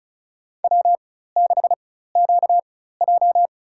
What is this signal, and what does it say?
Morse code, heard as a keyed tone, about 36 wpm, spaced as for 16 wpm, 700 Hz. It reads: W6QJ
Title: USA Call Signs - 4 Characters - 4x Character Spacing - 35wpm